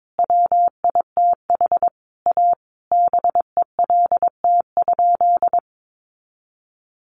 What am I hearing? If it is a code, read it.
Morse code, heard as a keyed tone, about 22 wpm, 700 Hz. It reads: WITH A BELT?
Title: Sentences from Top 1400 Words 22wpm